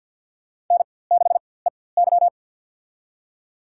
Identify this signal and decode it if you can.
Morse code, heard as a keyed tone, about 48 wpm, spaced as for 23 wpm, 700 Hz. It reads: N6EY
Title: USA Call Signs - 4x Character Spacing - 50wpm